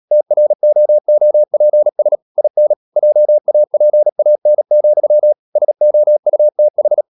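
Morse code, heard as a keyed tone, about 37 wpm, 600 Hz. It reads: TROOPS IN JAPAN, SOUTH
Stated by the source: Morse code practice at 35–39 wpm sending English text of Ken Follett's novel Never, from CW med Gnister.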